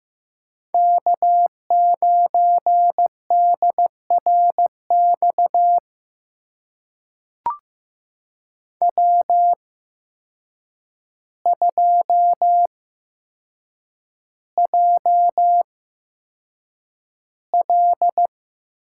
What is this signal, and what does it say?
Morse code, heard as a keyed tone, about 15 wpm, spaced as for 7 wpm, 700 Hz. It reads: K9DRX W2JL
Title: USA Call Signs - 5 Characters - 8x Character Spacing - 15wpm